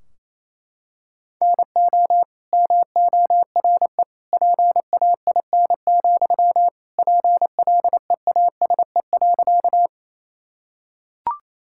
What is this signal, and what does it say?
Morse code, heard as a keyed tone, about 28 wpm, 700 Hz. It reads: NO MORE PAIN, PLEASE.
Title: Sentences from Top 1600 Words 28wpm